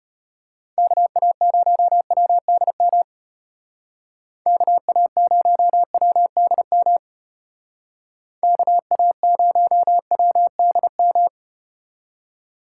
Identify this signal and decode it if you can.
Morse code, heard as a keyed tone, about 34 wpm, 700 Hz. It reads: KA0WDM KA0WDM KA0WDM